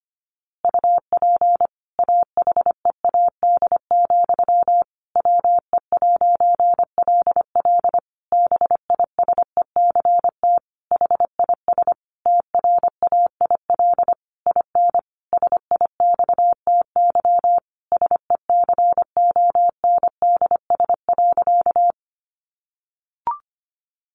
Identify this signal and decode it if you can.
Morse code, heard as a keyed tone, about 25 wpm, 700 Hz. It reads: UP AHEAD, WE'LL BISECT HIS TRAIL IN SIXTY SECONDS.
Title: A Sound Of Thunder 25wpm